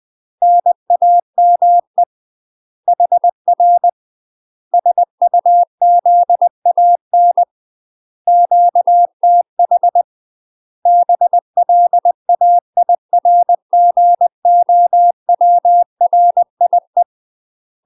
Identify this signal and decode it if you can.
Morse code, heard as a keyed tone, about 20 wpm, 700 Hz. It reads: NAME HR SUZAN QTH BLAIRGOWRIE
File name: UK MASTER QSOs TRAINING / 20WPM / UK MASTER QSOs 0055